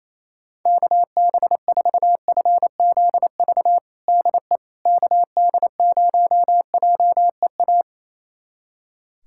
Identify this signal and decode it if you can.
Morse code, heard as a keyed tone, about 28 wpm, 700 Hz. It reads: KB4FZV DE KD0JEA
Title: USA Callsign DE Callsign - 6 Characters - 28wpm